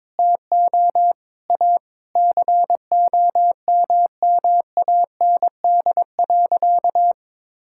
Morse code, heard as a keyed tone, about 22 wpm, 700 Hz. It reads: TO A COMMAND.
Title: Sentences from Top 1600 Words 22wpm